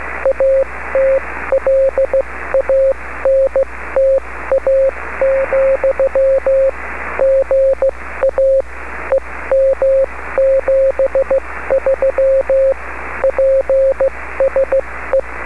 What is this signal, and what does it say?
Morse code, heard as a keyed tone, about 15 wpm, spaced as for 13 wpm, 540 Hz. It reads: ATLANTA, GA EM73 PSE
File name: beacons / kd4hlg-b